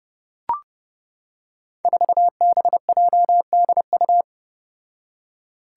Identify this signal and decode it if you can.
Morse code, heard as a keyed tone, about 31 wpm, 700 Hz. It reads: 4BJDU
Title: Five Challenging Letter-Number Combinations 30wpm